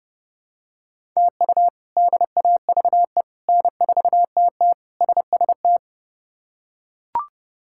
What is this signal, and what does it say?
Morse code, heard as a keyed tone, about 30 wpm, 700 Hz. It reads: TU DAVE N4TT SST
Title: K1USN SST QSO Practice - 30wpm